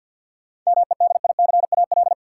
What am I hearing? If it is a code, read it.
Morse code, heard as a keyed tone, about 53 wpm, 700 Hz. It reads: MEDICAL
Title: Sentences from Top 800 Words - New Word Drill - 50wpm